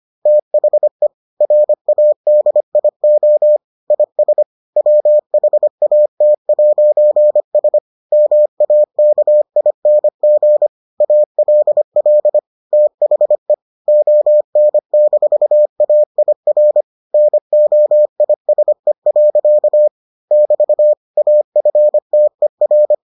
Morse code, heard as a keyed tone, about 25 wpm, 600 Hz. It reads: THE RADIO IS WHAT'S MAKING ALL THE ON-AIR NOISE. <BT> AFTER